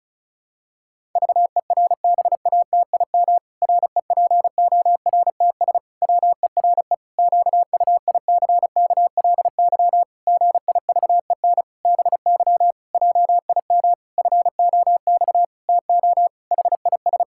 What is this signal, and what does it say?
Morse code, heard as a keyed tone, about 34 wpm, 700 Hz. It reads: VERBATIM REPORTS WERE QUICKLY GIVEN BY JIM FOX TO HIS
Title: Pangram Challenge v2 - 35wpm